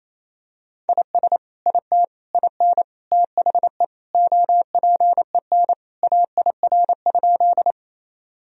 Morse code, heard as a keyed tone, about 28 wpm, 700 Hz. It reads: IS IT IN THE OPEN AIR?
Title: Sentences from Top 1000 Words - Review Part 1 - 28wpm